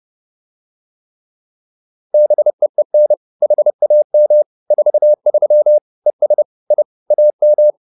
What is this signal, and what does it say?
Morse code, heard as a keyed tone, about 30 wpm, 600 Hz. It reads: BEEN HAM 43 ES I AM
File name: QSO310 30